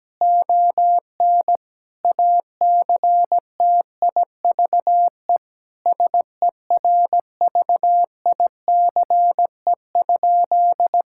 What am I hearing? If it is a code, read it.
Morse code, heard as a keyed tone, about 17 wpm, 700 Hz. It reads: ON ACTIVE SERVICE?
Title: Sentences from Top 1000 Words - Review Part 6 - 17wpm